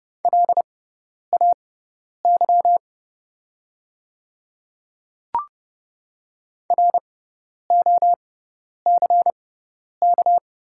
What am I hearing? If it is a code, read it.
Morse code, heard as a keyed tone, about 31 wpm, spaced as for 10 wpm, 700 Hz. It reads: LAY ROCK
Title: 6x Character Spacing - Top 500 Words - 30wpm